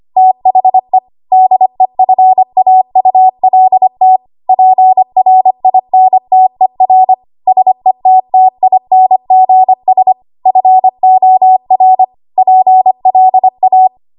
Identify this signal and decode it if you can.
Morse code, heard as a keyed tone, about 25 wpm, 750 Hz. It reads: THE DEFAULT PRINTER SETTINGS FOR PLA
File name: Archive / 25 WPM / 25WPM